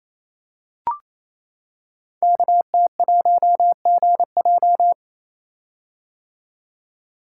Morse code, heard as a keyed tone, about 27 wpm, 700 Hz. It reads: KT1GJ
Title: Code Groups - 5 Letter:Numbers - 28wpm